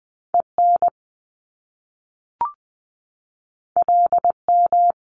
Morse code, heard as a keyed tone, about 20 wpm, 700 Hz. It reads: EN LM
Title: Top Two Letter Combinations in Words 20wpm